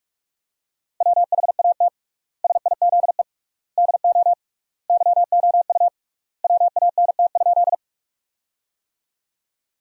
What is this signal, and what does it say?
Morse code, heard as a keyed tone, about 42 wpm, 700 Hz. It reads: WHAT SIZE DO YOU WANT?
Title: 3x Word-Spacing - Sentences from Top 400 Words - 45wpm